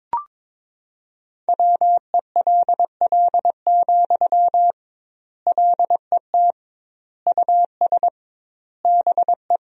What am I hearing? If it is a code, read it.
Morse code, heard as a keyed tone, about 22 wpm, 700 Hz. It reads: WELL, LET US BE
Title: 2x Word-Spacing - Sentences from Top 300 Words - 22wpm